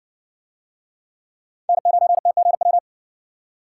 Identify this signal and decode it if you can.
Morse code, heard as a keyed tone, about 60 wpm, 700 Hz. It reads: N9TGW